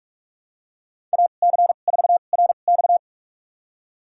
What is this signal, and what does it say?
Morse code, heard as a keyed tone, about 46 wpm, spaced as for 32 wpm, 700 Hz. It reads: AC4RX
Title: USA Call Signs - 5 Characters - 2x Character Spacing - 45wpm